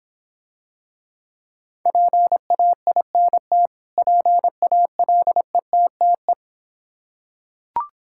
Morse code, heard as a keyed tone, about 26 wpm, 700 Hz. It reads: PAINT PALETTE